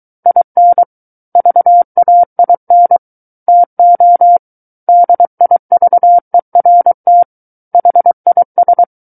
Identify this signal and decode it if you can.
Morse code, heard as a keyed tone, about 23 wpm, 700 Hz. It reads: IN VAIN TO DIVERT HIS
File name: Book One / 14-Chapter 14 - 23 WPM